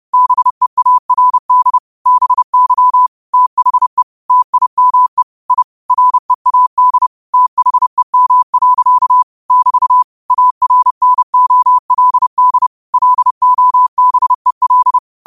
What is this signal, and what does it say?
Morse code, heard as a keyed tone, about 30 wpm, 1000 Hz. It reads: BEARD BY THE TIME I READ THEM. <BT> ARNOLD LOBEL
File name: qotdfiles / 30wpm